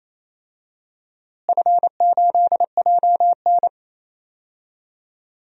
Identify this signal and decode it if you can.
Morse code, heard as a keyed tone, about 28 wpm, 700 Hz. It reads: F8JN
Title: DXCC-100 Call Signs - 4 Characters - Repeated 3x - 28wpm